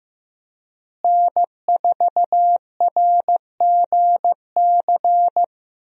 Morse code, heard as a keyed tone, about 15 wpm, 700 Hz. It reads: N4RGC
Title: USA Call Signs - 5 Characters - Repeated 3x - 15wpm